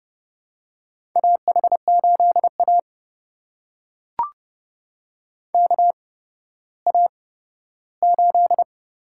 Morse code, heard as a keyed tone, about 31 wpm, spaced as for 15 wpm, 700 Hz. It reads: AH8A KA8
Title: USA Call Signs - 8x Character Spacing - 30wpm